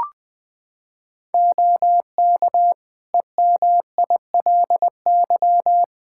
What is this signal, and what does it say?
Morse code, heard as a keyed tone, about 21 wpm, 700 Hz. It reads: OK EMILY
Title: Practice QSO - Exchange 4 - Ending - 20wpm